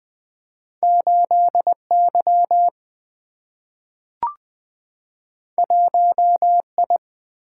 Morse code, heard as a keyed tone, about 20 wpm, 700 Hz. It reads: 8 Y 1 I
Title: Single-Letter Single-Number 20wpm